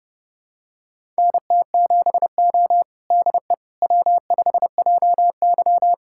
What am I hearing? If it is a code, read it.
Morse code, heard as a keyed tone, about 30 wpm, 700 Hz. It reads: NT7O DE W5JY
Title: USA Callsign DE Callsign - 4 Characters - 30wpm